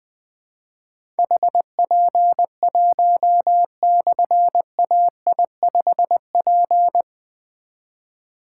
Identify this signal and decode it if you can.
Morse code, heard as a keyed tone, about 20 wpm, 700 Hz. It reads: HP1/AI5P
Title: Difficult Call Signs 20wpm